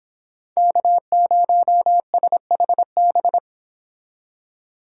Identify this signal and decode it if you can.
Morse code, heard as a keyed tone, about 30 wpm, 700 Hz. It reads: K0SHB